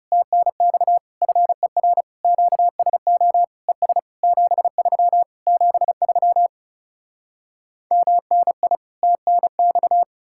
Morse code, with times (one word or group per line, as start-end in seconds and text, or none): 0.1-1.0 TNX
1.2-2.0 FER
2.2-3.4 QSO
3.7-4.0 ES
4.2-5.2 73
5.5-6.5 73
7.9-8.8 MNI
9.0-10.0 TNX